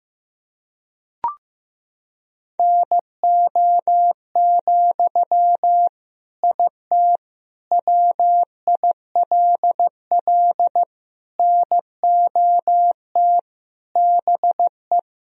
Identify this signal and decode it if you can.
Morse code, heard as a keyed tone, about 15 wpm, 700 Hz. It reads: NO, IT WILL NOT BE
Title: Sentences from Top 1000 Words - Review Part 4 - 15wpm